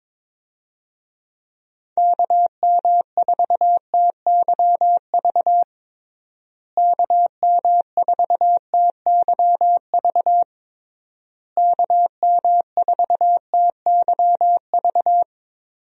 Morse code, heard as a keyed tone, about 22 wpm, 700 Hz. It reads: KM4TYV KM4TYV KM4TYV
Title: USA Call Signs - 6 Characters - Repeated 3x - 22wpm